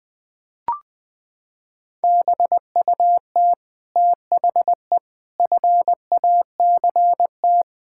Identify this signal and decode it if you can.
Morse code, heard as a keyed tone, about 20 wpm, 700 Hz. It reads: BUT THE FACT
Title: Sets of 8 Words - 20wpm